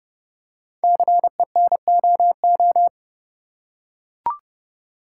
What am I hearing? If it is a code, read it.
Morse code, heard as a keyed tone, about 30 wpm, 690 Hz. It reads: CENOO
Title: Code Groups - 5 Letters - 30wpm